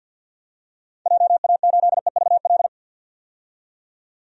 Morse code, heard as a keyed tone, about 53 wpm, 700 Hz. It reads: JA8EVL
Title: DXCC-100 Call Signs - 6 Characters - 50wpm